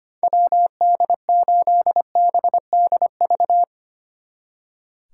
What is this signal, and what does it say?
Morse code, heard as a keyed tone, about 31 wpm, 700 Hz. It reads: WD8BDV
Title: Speed Racing - USA 2x3 Call Signs 25wpm